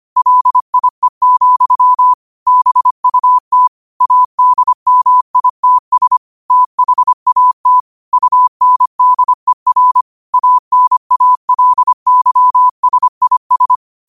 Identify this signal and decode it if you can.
Morse code, heard as a keyed tone, about 25 wpm, 1000 Hz. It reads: LIE, BUT ADMITS THAT UNDER ANALYSIS
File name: qotdfiles / 25wpm